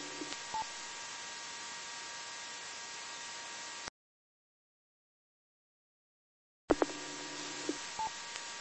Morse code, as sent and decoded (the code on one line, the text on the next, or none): none